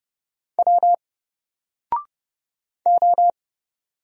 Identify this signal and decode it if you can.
Morse code, heard as a keyed tone, about 30 wpm, 700 Hz. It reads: W O
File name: Single-Letter-Number-30wpm